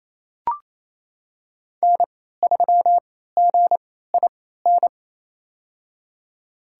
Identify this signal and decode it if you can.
Morse code, heard as a keyed tone, about 28 wpm, spaced as for 16 wpm, 700 Hz. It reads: N3GIN
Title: USA Call Signs - 3x Character Spacing - 28wpm